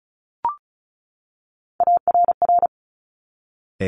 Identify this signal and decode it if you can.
Morse code, heard as a keyed tone, about 36 wpm, 700 Hz. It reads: ARR
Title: Top Three Letter Combinations in Words 35wpm